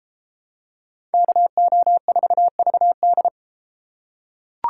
Morse code, heard as a keyed tone, about 30 wpm, 700 Hz. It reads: KO4VD